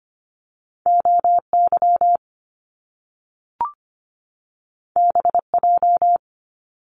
Morse code, heard as a keyed tone, about 25 wpm, 700 Hz. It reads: OY BJ
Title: Top Two Letter Combinations in Words 25wpm